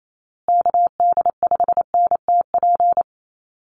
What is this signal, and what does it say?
Morse code, heard as a keyed tone, about 28 wpm, 700 Hz. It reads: KD5NTP
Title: USA Call Signs Repeated 2x 28wpm